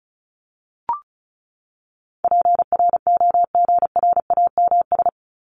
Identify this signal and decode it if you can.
Morse code, heard as a keyed tone, about 34 wpm, 700 Hz. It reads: PROGRAMS
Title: ES:S Suffix - Top 100 Words with Base Words 35wpm